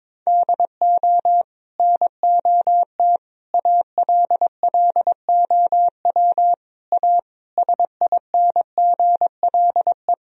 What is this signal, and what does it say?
Morse code, heard as a keyed tone, about 22 wpm, 700 Hz. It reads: DO NOT ALLOW A SINGLE
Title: Sentences from Top 1000 Words - Review Part 4 - 22wpm